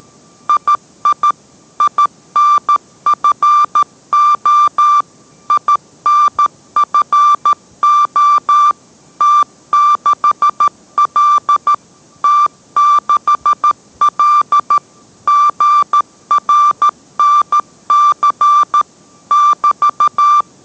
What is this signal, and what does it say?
Morse code, heard as a keyed tone, about 13 wpm, 1180 Hz. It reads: II INFO INFO T6L T6L GRNC <BT>